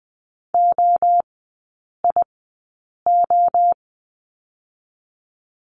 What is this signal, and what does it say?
Morse code, heard as a keyed tone, about 20 wpm, spaced as for 8 wpm, 700 Hz. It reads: OIO
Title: 035 Lesson 7e - Letter s - Three Characters 20wpm